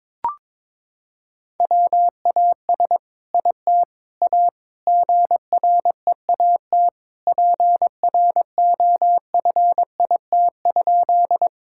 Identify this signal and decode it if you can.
Morse code, heard as a keyed tone, about 22 wpm, 700 Hz. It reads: WAS IT A GREAT PROFIT?
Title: Sentences from Top 1000 Words - Review Part 5 - 22wpm